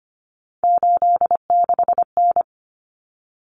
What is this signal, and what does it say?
Morse code, heard as a keyed tone, about 25 wpm, 700 Hz. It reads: 86N
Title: Common Cut Numbers - 3-Digit 25wpm